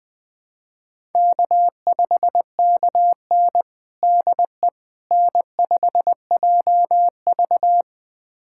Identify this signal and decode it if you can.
Morse code, heard as a keyed tone, about 20 wpm, 700 Hz. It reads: K5KN DE N5JV
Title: USA Callsign DE Callsign - 4 Characters - 20wpm